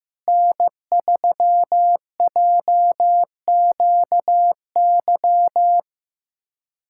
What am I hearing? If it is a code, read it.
Morse code, heard as a keyed tone, about 15 wpm, 700 Hz. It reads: N3JQY
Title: USA Callsign DE Callsign - 5 Characters - 15wpm